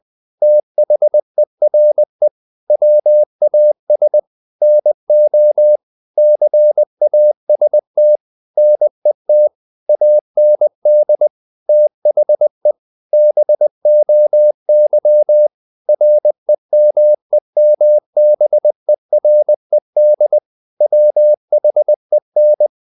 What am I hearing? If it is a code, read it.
Morse code, heard as a keyed tone, about 20 wpm, 600 Hz. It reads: THERE WAS NO CAST NET AND THE BOY REMEMBERED WHEN
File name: the old man and the sea / the old man and the sea part 1 20wpm